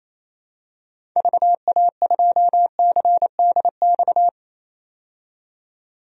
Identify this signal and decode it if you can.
Morse code, heard as a keyed tone, about 27 wpm, 700 Hz. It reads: VA2CDX
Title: DXCC-100 Call Signs - 6 Characters - 28wpm